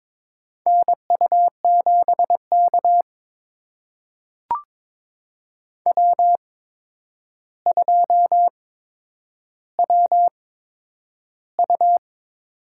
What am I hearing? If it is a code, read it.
Morse code, heard as a keyed tone, about 22 wpm, spaced as for 10 wpm, 700 Hz. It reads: NU7K W2WU
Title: USA Call Signs - 4 Characters - 8x Character Spacing - 22wpm